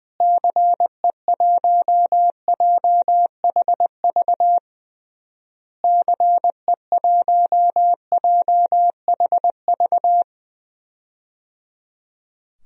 Morse code, heard as a keyed tone, about 20 wpm, 700 Hz. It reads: CE1JHV CE1JHV
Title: DXCC-100 Call Signs - 6 Characters - Repeated 3x - 20wpm